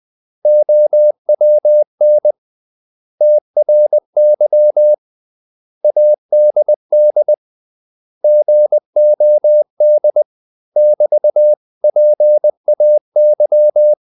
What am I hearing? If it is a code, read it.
Morse code, heard as a keyed tone, about 20 wpm, 600 Hz. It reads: OWN TRY ADD GOD <BT>PAY
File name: CW / Practice / 500MostCommonEnglishWords20